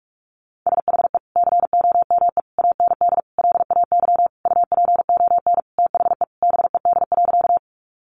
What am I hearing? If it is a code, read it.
Morse code, heard as a keyed tone, about 45 wpm, 700 Hz. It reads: SHE COME AND LAY UPON THE BED.
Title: Sentences from Top 1100 Words 45wpm